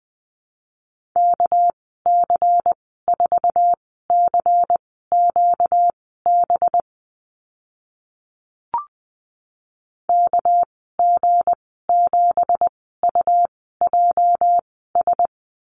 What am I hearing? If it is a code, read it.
Morse code, heard as a keyed tone, about 20 wpm, spaced as for 14 wpm, 700 Hz. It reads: KC4CQB KG7UJS
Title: 245 Lesson 39g - Letter z - Call Signs 20wpm